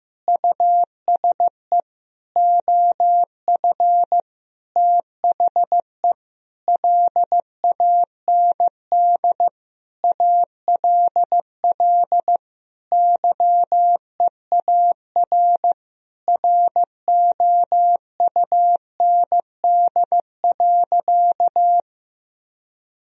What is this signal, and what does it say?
Morse code, heard as a keyed tone, about 15 wpm, 700 Hz. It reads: USE OF THE LAND ALL YEAR ROUND.
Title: Sentences from Top 1000 Words - Review Part 1 - 15wpm